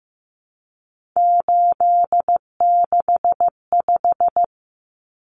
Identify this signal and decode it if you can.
Morse code, heard as a keyed tone, about 15 wpm, 700 Hz. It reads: 865
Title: Common Cut Numbers - 3-Digit 15wpm